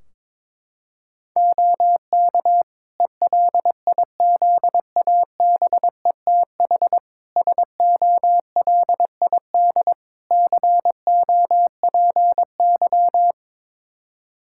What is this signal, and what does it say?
Morse code, heard as a keyed tone, about 22 wpm, 700 Hz. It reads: OK ELIZABETH SOLID COPY
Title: Practice QSO - Exchange 3 - More Info - 22wpm